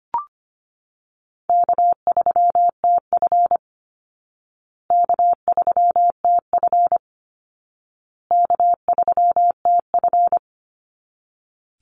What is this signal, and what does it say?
Morse code, heard as a keyed tone, about 25 wpm, 700 Hz. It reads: K3TF K3TF K3TF
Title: Call Signs Repeated 2x - 25wpm